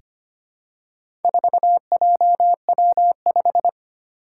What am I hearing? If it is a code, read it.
Morse code, heard as a keyed tone, about 25 wpm, 700 Hz. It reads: 4JW5
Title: Four Challenging Letter-Number Combinations 25wpm